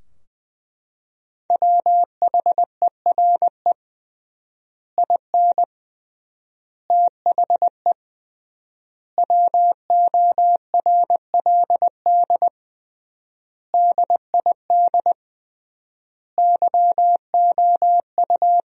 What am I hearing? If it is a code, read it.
Morse code, heard as a keyed tone, about 20 wpm, 700 Hz. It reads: WHERE IN THE WORLD DID YOU
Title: 3x Word-Spacing - Sentences from Top 200 Words - 20wpm